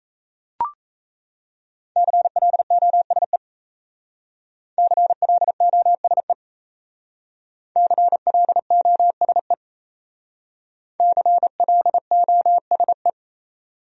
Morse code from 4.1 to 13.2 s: CLOSE CLOSE CLOSE

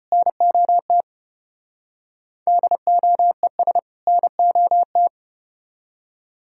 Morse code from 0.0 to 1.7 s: NOT

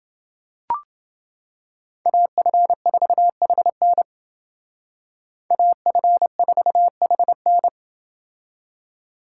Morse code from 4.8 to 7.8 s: AF4HN

